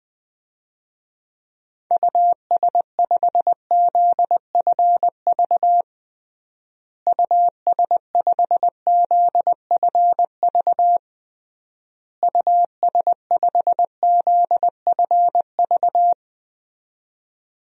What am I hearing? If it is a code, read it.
Morse code, heard as a keyed tone, about 20 wpm, 700 Hz. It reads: US5ZFV US5ZFV US5ZFV
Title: DXCC-100 Call Signs - 6 Characters - Repeated 3x - 20wpm